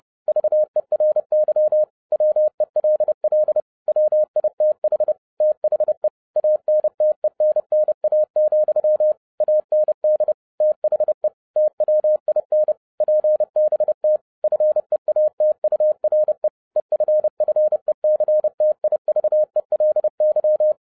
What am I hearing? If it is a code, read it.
Morse code, heard as a keyed tone, about 30 wpm, 610 Hz. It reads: VERY WELL WITH THE ANTENNA, AND THE TWIN PBT FEATURE EFFECTIVELY
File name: 30wpm